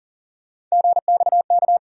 Morse code, heard as a keyed tone, about 40 wpm, 700 Hz. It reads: GXK
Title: USA Call Sign Postfixes - 3 Letters 40wpm